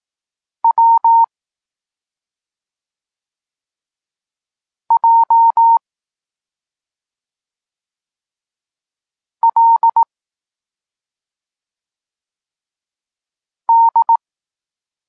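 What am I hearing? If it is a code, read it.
Morse code, heard as a keyed tone, about 18 wpm, 920 Hz. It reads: W J L D
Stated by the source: Polish Morse lesson teaching the letter C, from SP9MOA Niepołomicki Klub Krótkofalowców.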